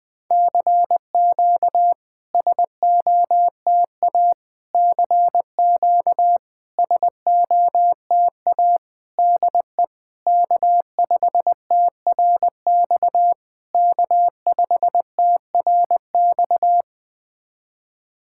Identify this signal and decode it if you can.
Morse code, heard as a keyed tone, about 20 wpm, 690 Hz. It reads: CQ SOTA CQ SOTA DE K5TRX K5TRX
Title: SOTA QSO Practice 20wpm